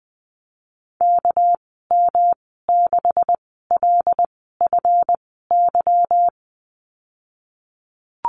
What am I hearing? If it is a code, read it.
Morse code, heard as a keyed tone, about 20 wpm, spaced as for 15 wpm, 700 Hz. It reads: KM6LFY
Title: 238 Lesson 38g - Letter q - Call Signs 20wpm